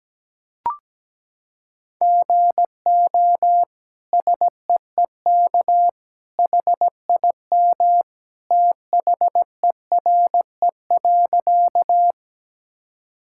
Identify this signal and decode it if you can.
Morse code, heard as a keyed tone, about 17 wpm, 700 Hz. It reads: GO SEEK HIM THERE.
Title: Sentences from Top 1600 Words 17wpm